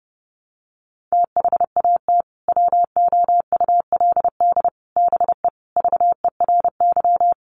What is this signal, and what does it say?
Morse code, heard as a keyed tone, about 30 wpm, 700 Hz. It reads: THAT WOULD BE VERY
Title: Sentences from Top 200 Words 30wpm